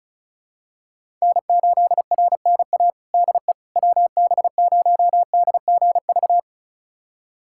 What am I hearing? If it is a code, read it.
Morse code, heard as a keyed tone, about 34 wpm, 700 Hz. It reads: N8RNA DE WB0DGV
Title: USA Callsign DE Callsign 35wpm